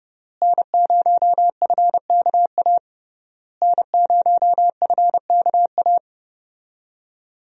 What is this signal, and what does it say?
Morse code, heard as a keyed tone, about 32 wpm, 700 Hz. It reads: N0FKA N0FKA
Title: USA Call Signs - 5 Characters - Repeated 3x - 30wpm